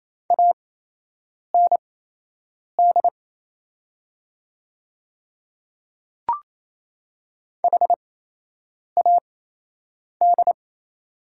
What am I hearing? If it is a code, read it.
Morse code, heard as a keyed tone, about 28 wpm, spaced as for 8 wpm, 700 Hz. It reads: AND HAD
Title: 8x Character Spacing - Top 100 Words - 28wpm